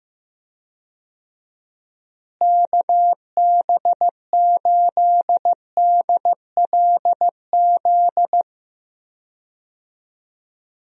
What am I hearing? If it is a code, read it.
Morse code, heard as a keyed tone, about 15 wpm, 700 Hz. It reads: KB8DLZ
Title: USA Call Signs - 6 Characters - Repeated 3x - 15wpm